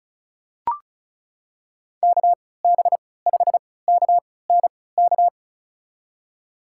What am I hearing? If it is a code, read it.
Morse code, heard as a keyed tone, about 36 wpm, spaced as for 20 wpm, 700 Hz. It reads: KB5KNK
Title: USA Call Signs - 6 Characters - 3x Character Spacing - 35wpm